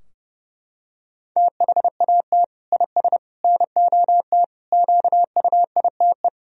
Morse code, heard as a keyed tone, about 30 wpm, 700 Hz. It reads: THAT IS NOT QUITE
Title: Sentences from Top 1000 Words - Review Part 3 - 30wpm